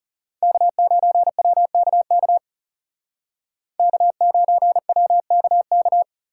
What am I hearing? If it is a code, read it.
Morse code, heard as a keyed tone, about 39 wpm, 700 Hz. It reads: K9WKK K9WKK